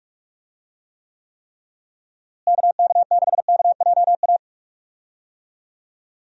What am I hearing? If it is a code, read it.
Morse code, heard as a keyed tone, about 41 wpm, 700 Hz. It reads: KK6KJA
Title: USA Call Signs - 6 Characters - Repeated 3x - 45wpm